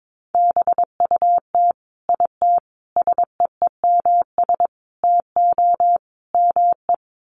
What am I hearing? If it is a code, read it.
Morse code, heard as a keyed tone, about 22 wpm, 700 Hz. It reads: BUT IT SEEMS TO ME
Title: Sets of 5 Words Encore2 22wpm